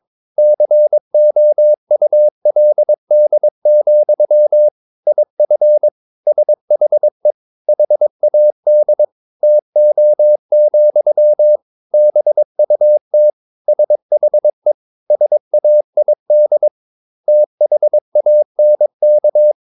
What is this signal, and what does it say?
Morse code, heard as a keyed tone, about 22 wpm, 600 Hz. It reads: COULD, IF SHE HAD TO, BUT SHE SAID THANK
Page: Romanen Never af Ken Follet på engelsk.